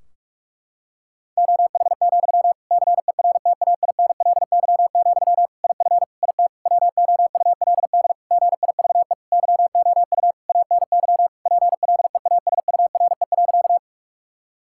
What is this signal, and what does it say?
Morse code, heard as a keyed tone, about 46 wpm, 700 Hz. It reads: OH, CERTAINLY, IF IT WOULD GIVE YOU ANY PLEASURE.
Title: Sentences from Top 1600 Words 45wpm